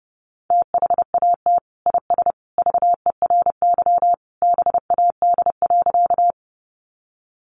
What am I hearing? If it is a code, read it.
Morse code, heard as a keyed tone, about 30 wpm, 700 Hz. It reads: THAT IS VERY BAD.
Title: Sentences from Top 700 Words 30wpm